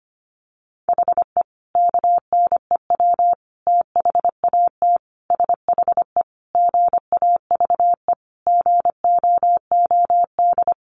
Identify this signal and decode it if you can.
Morse code, heard as a keyed tone, about 25 wpm, 700 Hz. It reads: HE KNEW THAT SHE GAVE GOOD
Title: Sentences from Top 1100 Words 25wpm